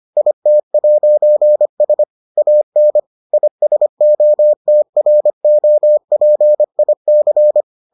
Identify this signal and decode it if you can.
Morse code, heard as a keyed tone, about 25 wpm, 600 Hz. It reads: IT'S AN ISOTROPIC